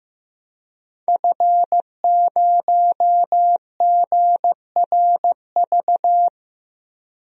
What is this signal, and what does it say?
Morse code, heard as a keyed tone, about 15 wpm, 700 Hz. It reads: F0GRV
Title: DXCC-100 Call Signs - 5 Characters - Repeated 2x - 15wpm